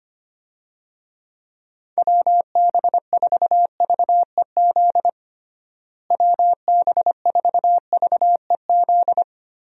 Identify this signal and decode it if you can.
Morse code, heard as a keyed tone, about 25 wpm, 700 Hz. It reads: WB4VEZ WB4VEZ
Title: USA Call Signs - 6 Characters - Repeated 2x - 25wpm